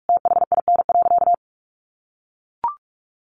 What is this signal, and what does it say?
Morse code, heard as a keyed tone, about 44 wpm, 700 Hz. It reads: THIN.
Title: Sentences from Top 1200 Words 45wpm